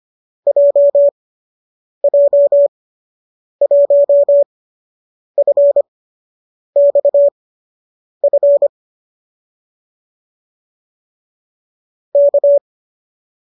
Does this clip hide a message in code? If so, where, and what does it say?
Yes: Morse code from 0.5 to 12.6 s, heard as a keyed tone, about 25 wpm, spaced as for 8 wpm, 570 Hz. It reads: JJ1FXF K